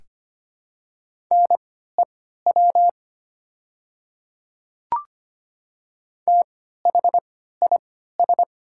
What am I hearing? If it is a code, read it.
Morse code, heard as a keyed tone, about 25 wpm, spaced as for 14 wpm, 700 Hz. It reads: NEW THIS
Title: 3x Character Spacing - Top 200 Words - 25wpm